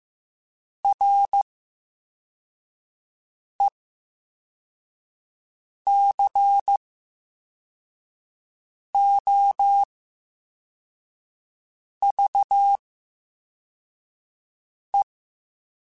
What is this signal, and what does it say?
Morse code, heard as a keyed tone, about 15 wpm, spaced as for 4 wpm, 800 Hz. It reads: RECOVE
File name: Special INGs-mp3s / Special INGs-mp3s / VerySlowToFaster Ing @13